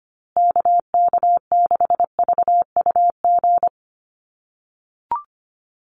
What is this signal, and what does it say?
Morse code, heard as a keyed tone, about 25 wpm, 700 Hz. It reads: KK6VUG
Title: USA Call Signs Repeated 3x 25wpm